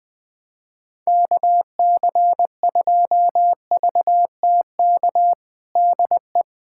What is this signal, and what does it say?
Morse code, heard as a keyed tone, about 21 wpm, 700 Hz. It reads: KC2VTK DE
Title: USA Callsign DE Callsign 20wpm